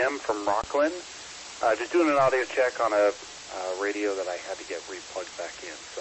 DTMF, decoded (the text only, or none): none